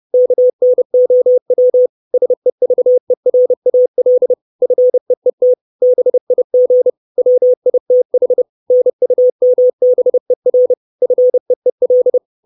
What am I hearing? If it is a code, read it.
Morse code, heard as a keyed tone, about 30 wpm, 500 Hz. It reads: KNOW SEVERAL FEET BIG WITH NUMBER FEEL
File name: 632 Words 30WPM